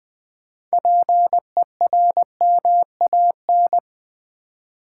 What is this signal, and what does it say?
Morse code, heard as a keyed tone, about 20 wpm, 700 Hz. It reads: PERMAN
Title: Build A Word - 3 Letters At A Time - 9-Letter Words - 20wpm